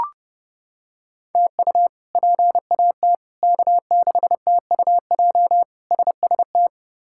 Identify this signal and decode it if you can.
Morse code, heard as a keyed tone, about 30 wpm, 700 Hz. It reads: TU PAT K6TUJ SST
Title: K1USN SST QSO Practice - 30wpm